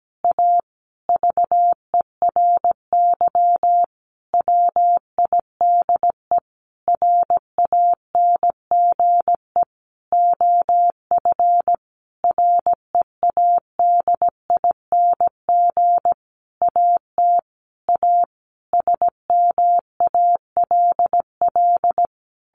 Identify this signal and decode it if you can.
Morse code, heard as a keyed tone, about 17 wpm, 700 Hz. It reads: A VERY WIDE RANGE OF READING AT A SMALL
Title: Sentences from Top 900 Words 17wpm